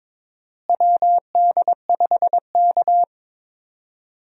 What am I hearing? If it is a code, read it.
Morse code, heard as a keyed tone, about 22 wpm, 700 Hz. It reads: WD5K
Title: USA Call Signs - 4 Characters - Repeated 3x - 22wpm